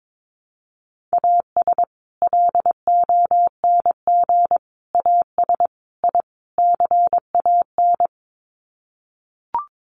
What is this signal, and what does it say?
Morse code, heard as a keyed tone, about 22 wpm, 700 Hz. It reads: AS LONG AS I CAN
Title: Sets of 5 Words Encore2 22wpm